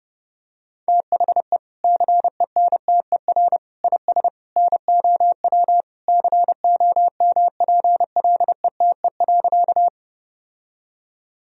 Morse code, heard as a keyed tone, about 30 wpm, 700 Hz. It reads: THE CENTER IS NOW COMPLETE.
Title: Sentences from Top 1000 Words - Review Part 2 - 30wpm